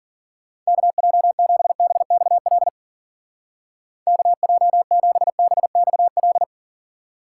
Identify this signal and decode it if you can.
Morse code, heard as a keyed tone, about 50 wpm, 700 Hz. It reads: KJ7BXL KJ7BXL